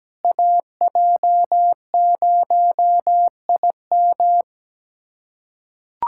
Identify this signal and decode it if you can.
Morse code, heard as a keyed tone, about 17 wpm, 700 Hz. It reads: AJ0IM
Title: USA Callsign DE Callsign - 5 Characters - 17wpm